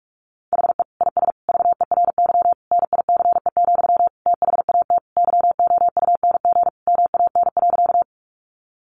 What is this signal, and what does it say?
Morse code, heard as a keyed tone, about 44 wpm, 700 Hz. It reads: HE IS VERY NICE, THAT YOUNG MAN.
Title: Sentences from Top 1100 Words 45wpm